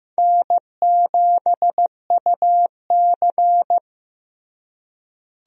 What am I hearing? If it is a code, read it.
Morse code, heard as a keyed tone, about 16 wpm, 700 Hz. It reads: N7UC
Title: USA Call Signs - 4 Characters - Repeated 2x - 15wpm